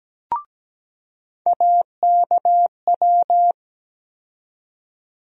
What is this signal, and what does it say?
Morse code, heard as a keyed tone, about 17 wpm, 700 Hz. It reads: AKW
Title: USA Call Sign Postfixes - 3 Letters 17wpm